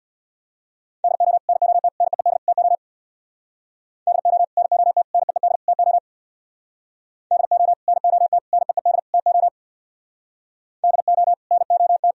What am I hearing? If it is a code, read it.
Morse code, heard as a keyed tone, about 64 wpm, 700 Hz. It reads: DO NOT NEED TO DO NOT NEED TO DO NOT NEED TO DO NOT